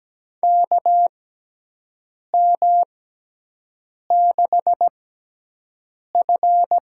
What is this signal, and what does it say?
Morse code, heard as a keyed tone, about 17 wpm, spaced as for 6 wpm, 700 Hz. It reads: KM6F
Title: USA Call Signs - 6x Character Spacing - 17wpm